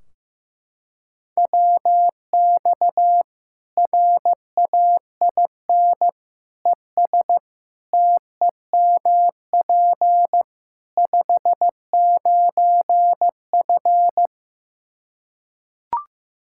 Morse code, heard as a keyed tone, about 15 wpm, 700 Hz. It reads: WX RAIN ES TEMP 59F
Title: Practice QSO - Exchange 2 - Info - 15wpm